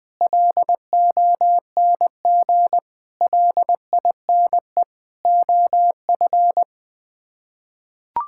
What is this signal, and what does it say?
Morse code, heard as a keyed tone, about 20 wpm, 700 Hz. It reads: LONG LINE OF
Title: Sets of 8 Words - 20wpm